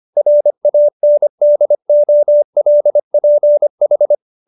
Morse code, heard as a keyed tone, about 25 wpm, 590 Hz. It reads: RANDOLPH